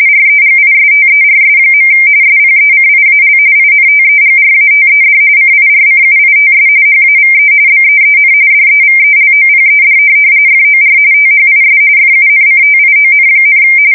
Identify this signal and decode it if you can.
RTTY, about 45.45 baud. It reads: REGISTERS. SPECIAL INTEGRATED CIRCUITS HAVE BEEN DEVELOPED FOR THIS FUNCTION,
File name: fsk170 / teletype-RTTY